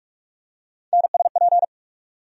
Morse code, heard as a keyed tone, about 46 wpm, 700 Hz. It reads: NSP
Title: USA Call Sign Postfixes - 3 Letters 45wpm